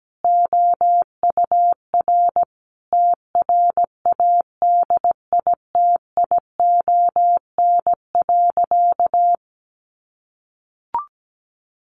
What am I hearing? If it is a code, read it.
Morse code, heard as a keyed tone, about 17 wpm, 700 Hz. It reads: OUR TRADITION.
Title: Sentences from Top 1500 Words 17wpm